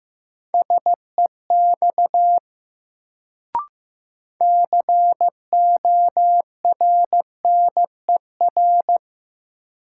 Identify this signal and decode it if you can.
Morse code, heard as a keyed tone, about 15 wpm, 700 Hz. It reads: SEX CORNER
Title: Sentences from Top 1600 Words - New Word Drill Alt - 15wpm